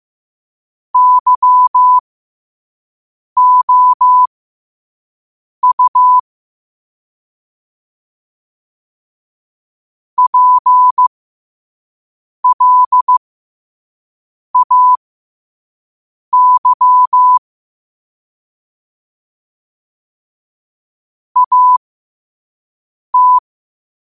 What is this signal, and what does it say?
Morse code, heard as a keyed tone, about 15 wpm, spaced as for 5 wpm, 1000 Hz. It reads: YOU PLAY AT
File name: qotdfiles / 5wpm